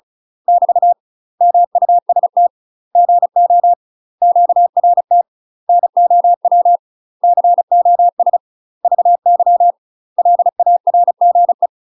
Morse code, heard as a keyed tone, about 35 wpm, 700 Hz. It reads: <BT> MUST GO QRT NOW COS VY LARGE
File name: UK MASTER QSOs TRAINING/35WPM